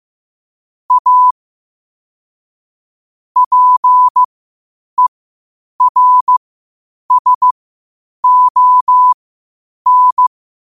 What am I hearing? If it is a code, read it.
Morse code, heard as a keyed tone, about 15 wpm, spaced as for 8 wpm, 1000 Hz. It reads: A PERSON
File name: qotdfiles / 8wpm